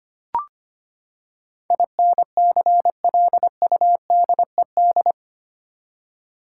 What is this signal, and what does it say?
Morse code, heard as a keyed tone, about 25 wpm, 700 Hz. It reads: INCLUDED
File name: Eight Letter Words - 25wpm